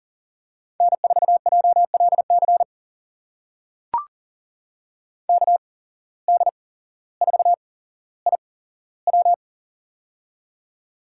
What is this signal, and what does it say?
Morse code, heard as a keyed tone, about 40 wpm, spaced as for 17 wpm, 700 Hz. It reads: N4JLC KD4IW
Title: USA Call Signs - 5 Characters - 8x Character Spacing - 40wpm